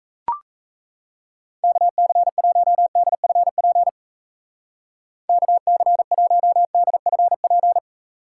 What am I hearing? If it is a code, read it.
Morse code, heard as a keyed tone, about 37 wpm, 700 Hz. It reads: KC1DFP KC1DFP